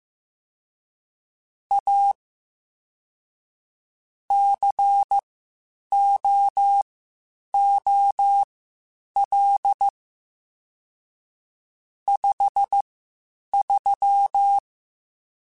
Morse code, heard as a keyed tone, about 15 wpm, spaced as for 8 wpm, 800 Hz. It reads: A COOL 53
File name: Spaced Characters / Spaced Characters-mp3s / Weather5 1 sp @13